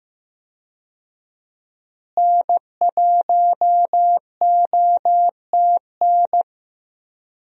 Morse code, heard as a keyed tone, about 16 wpm, 700 Hz. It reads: N1OTN
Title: USA Call Signs - 5 Characters - Repeated 2x - 15wpm